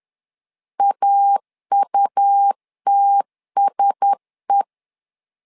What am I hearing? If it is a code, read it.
Morse code, heard as a keyed tone, about 10 wpm, 800 Hz. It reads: AUTSE
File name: LEZIONI / 4 Q1S 10wpm 2133